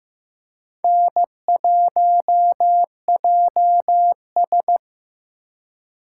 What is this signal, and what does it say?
Morse code, heard as a keyed tone, about 15 wpm, 700 Hz. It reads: N1JS
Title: USA Call Signs - 4 Characters - 4x Character Spacing - 15wpm